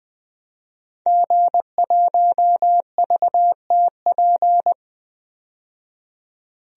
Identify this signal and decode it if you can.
Morse code, heard as a keyed tone, about 20 wpm, 700 Hz. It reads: G1VTP
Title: DXCC-100 Call Signs - 5 Characters - 20wpm